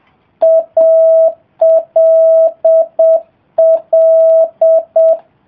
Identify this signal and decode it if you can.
Morse code, heard as a keyed tone, about 7 wpm, 660 Hz. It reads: ALL